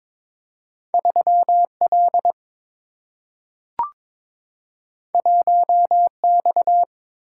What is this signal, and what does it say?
Morse code, heard as a keyed tone, about 22 wpm, 700 Hz. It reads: 3 L 1 X
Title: Single-Letter Single-Number 22wpm